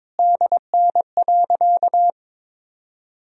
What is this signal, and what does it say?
Morse code, heard as a keyed tone, about 22 wpm, 700 Hz. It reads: DN.
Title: Sentences from Top 500 Words and Abbreviations 22wpm